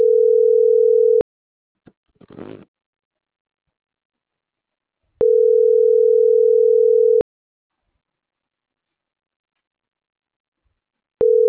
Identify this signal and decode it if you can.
Ringback tone (North America), 440 + 480 Hz.